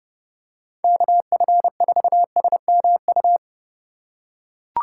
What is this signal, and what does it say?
Morse code, heard as a keyed tone, about 29 wpm, 700 Hz. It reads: KF4SMU